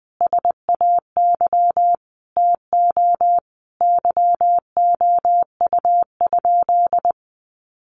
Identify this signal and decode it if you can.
Morse code, heard as a keyed tone, about 20 wpm, 700 Hz. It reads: SAY TO YOU?
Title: Sentences from Top 800 Words 20wpm